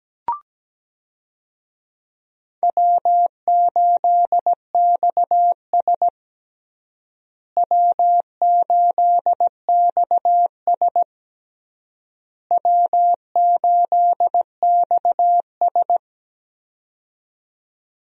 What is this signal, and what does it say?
Morse code, heard as a keyed tone, about 17 wpm, 700 Hz. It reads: W8XS W8XS W8XS
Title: USA Call Signs - 4 Characters - Repeated 3x - 17wpm